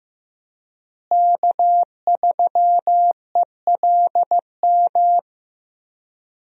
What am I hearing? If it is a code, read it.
Morse code, heard as a keyed tone, about 16 wpm, 700 Hz. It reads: K3ELM
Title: USA Call Signs - 5 Characters - Repeated 2x - 15wpm